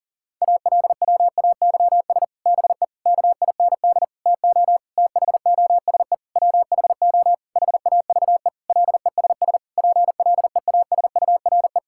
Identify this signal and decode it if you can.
Morse code, heard as a keyed tone, about 40 wpm, 700 Hz. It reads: ALWAYS BE KIND TO THOSE WHO HAVE LESS PLEASURE
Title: Sentences from Top 1600 Words 40wpm